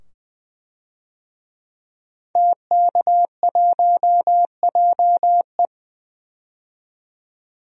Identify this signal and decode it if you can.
Morse code, heard as a keyed tone, about 20 wpm, 700 Hz. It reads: TK1JE
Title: DXCC-100 Call Signs - 5 Characters - 20wpm